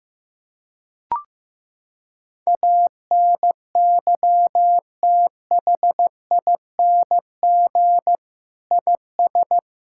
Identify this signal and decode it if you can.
Morse code, heard as a keyed tone, about 14 wpm, 700 Hz. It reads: ANYTHING IS
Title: Sentences from Top 1000 Words - Review Part 6 - 15wpm